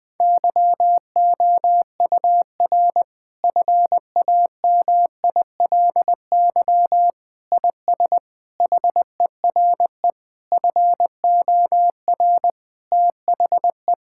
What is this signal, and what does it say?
Morse code, heard as a keyed tone, about 21 wpm, 700 Hz. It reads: YOUR FAMILY IS HERE FOR THE
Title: Sentences from Top 1600 Words 20wpm